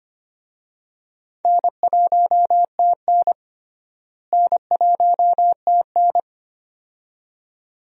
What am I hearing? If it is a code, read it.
Morse code, heard as a keyed tone, about 25 wpm, 700 Hz. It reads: N1TN N1TN
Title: USA Call Signs - 4 Characters - Repeated 2x - 25wpm